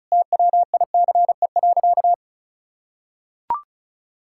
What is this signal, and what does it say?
Morse code, heard as a keyed tone, about 36 wpm, 700 Hz. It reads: TWICE.
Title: Sentences from Top 1600 Words 35wpm